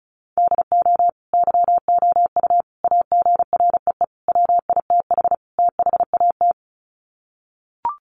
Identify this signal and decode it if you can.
Morse code, heard as a keyed tone, about 34 wpm, 700 Hz. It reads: DO YOU AGREE WITH THAT
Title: Sets of 5 Words Encore 35wpm